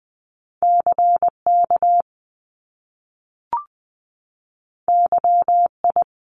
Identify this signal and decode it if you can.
Morse code, heard as a keyed tone, about 20 wpm, 700 Hz. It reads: C K Y I
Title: Top Two Letter Combinations in Words 20wpm